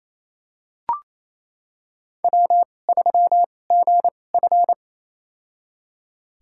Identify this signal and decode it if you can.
Morse code, heard as a keyed tone, about 28 wpm, spaced as for 20 wpm, 700 Hz. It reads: W3GF
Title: USA Call Signs - 4 Characters - 2x Character Spacing - 28wpm